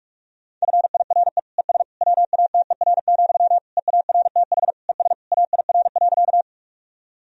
Morse code, heard as a keyed tone, about 41 wpm, 700 Hz. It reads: FIRE ES WATER, EARTH ES AIR.
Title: Sentences from Top 400 Words and Abbreviations 45wpm